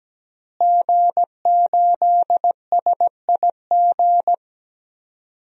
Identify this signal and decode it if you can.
Morse code, heard as a keyed tone, about 17 wpm, 700 Hz. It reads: G8SIG
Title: DXCC-100 Call Signs - 5 Characters - Repeated 3x - 17wpm